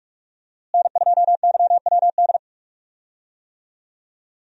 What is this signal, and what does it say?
Morse code, heard as a keyed tone, about 41 wpm, 700 Hz. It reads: N2YWD
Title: USA Call Signs - 5 Characters - Repeated 3x - 45wpm